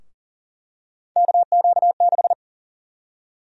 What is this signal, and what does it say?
Morse code, heard as a keyed tone, about 37 wpm, 700 Hz. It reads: KQ6
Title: USA Call Sign Prefixes - Repeated 2x - 40wpm